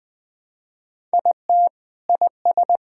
Morse code, heard as a keyed tone, about 20 wpm, 700 Hz. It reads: IT IS
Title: Build A Phrase - 3-Word Phrases - 20wpm